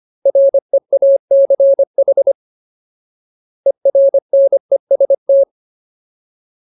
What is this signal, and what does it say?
Morse code, heard as a keyed tone, about 25 wpm, 550 Hz. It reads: REACH ERNEST
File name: Ultimate Word List / 25WPM Ultimate